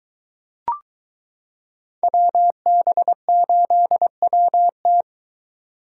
Morse code, heard as a keyed tone, about 23 wpm, 700 Hz. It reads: WB8WT